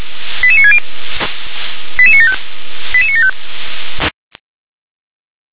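Five-tone selective call (ZVEI), heard as quick runs of five tones, tones of about 70 ms. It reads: 88960 88975 88975
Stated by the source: ZVEI five-tone selective call 88965, received on an SDR.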